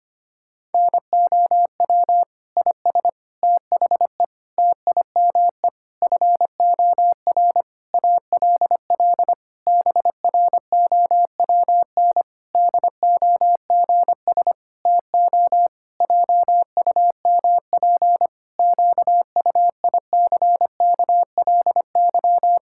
Morse code, heard as a keyed tone, about 25 wpm, 700 Hz. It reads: NOW IS THE TIME FOR ALL BROWN DOGS TO JUMP QUICKLY
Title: Pangram Challenge v2 - 25wpm